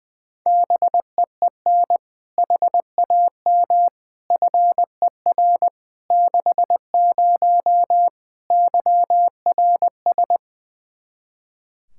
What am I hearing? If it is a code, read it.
Morse code, heard as a keyed tone, about 20 wpm, 700 Hz. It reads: BEEN HAM FER 60 YRS
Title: K1USN SST QSO Practice 20wpm